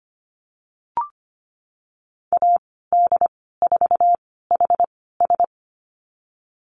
Morse code, heard as a keyed tone, about 25 wpm, spaced as for 16 wpm, 700 Hz. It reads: AD4HS
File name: Lesson 15g - Number 5 - Call Signs 25wpm